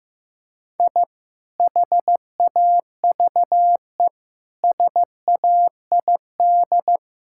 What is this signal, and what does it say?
Morse code, heard as a keyed tone, about 15 wpm, 700 Hz. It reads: I HAVE SAID
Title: Sentences from Top 1000 Words - Review Part 6 - 15wpm